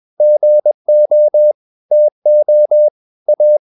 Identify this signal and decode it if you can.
Morse code, heard as a keyed tone, about 21 wpm, 600 Hz. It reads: GO TO A
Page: CW med Gnister